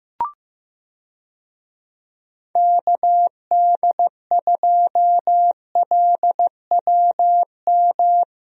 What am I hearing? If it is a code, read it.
Morse code, heard as a keyed tone, about 15 wpm, 700 Hz. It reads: KD2LWM
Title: USA Call Signs - 6 Characters - Repeated 3x - 15wpm